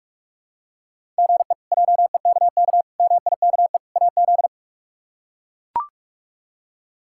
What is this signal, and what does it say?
Morse code, heard as a keyed tone, about 41 wpm, 700 Hz. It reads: GE JEKK MIKE AZ
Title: K1USN SST QSO Practice - 45wpm